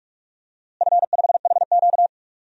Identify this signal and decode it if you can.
Morse code, heard as a keyed tone, about 41 wpm, 700 Hz. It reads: F5HQ
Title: DXCC-100 Call Signs - 4 Characters - Repeated 2x - 45wpm